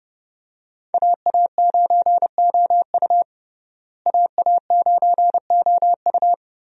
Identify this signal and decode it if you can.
Morse code, heard as a keyed tone, about 30 wpm, 700 Hz. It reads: AA9OU AA9OU
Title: USA Call Signs - 5 Characters - Repeated 3x - 30wpm